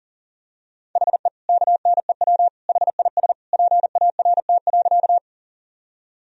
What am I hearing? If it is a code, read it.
Morse code, heard as a keyed tone, about 40 wpm, 700 Hz. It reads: HE KNEW HIS PART.
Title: Sentences from Top 1000 Words - Review Part 1 - 40wpm